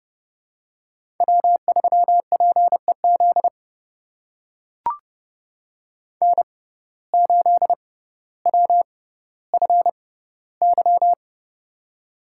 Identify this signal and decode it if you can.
Morse code, heard as a keyed tone, about 30 wpm, spaced as for 15 wpm, 700 Hz. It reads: W3PEZ N8WFY
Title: USA Call Signs - 5 Characters - 6x Character Spacing - 30wpm